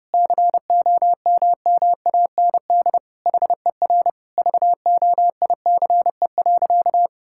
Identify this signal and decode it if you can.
Morse code, heard as a keyed tone, about 32 wpm, 700 Hz. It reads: COMMAND HER VOICE.
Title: Sentences from Top 1600 Words 30wpm